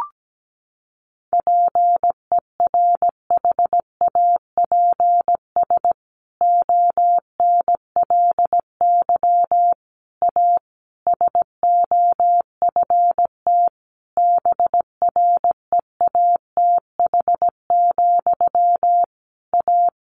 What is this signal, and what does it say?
Morse code, heard as a keyed tone, about 17 wpm, 700 Hz. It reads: PERHAPS ONLY A SOFT BREATH, A
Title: A Sound Of Thunder 17wpm